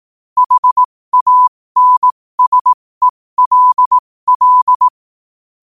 Morse code, heard as a keyed tone, about 18 wpm, spaced as for 15 wpm, 1000 Hz. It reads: HANSELL
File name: qotdfiles / 15wpm